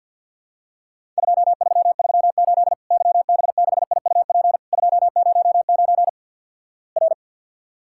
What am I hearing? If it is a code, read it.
Morse code, heard as a keyed tone, about 53 wpm, 700 Hz. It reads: 2338 YB6IUP 209 R
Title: Worked All Europe DX Contest - QTC Exchange - 50wpm